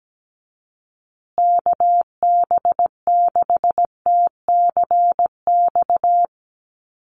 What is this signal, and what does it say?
Morse code, heard as a keyed tone, about 17 wpm, 700 Hz. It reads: KB6TCX
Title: USA Call Signs Repeated 2x 17wpm